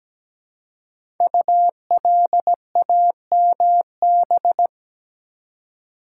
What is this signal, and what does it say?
Morse code, heard as a keyed tone, about 17 wpm, 700 Hz. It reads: ULAMB
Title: Code Groups - 5 Letters - 17wpm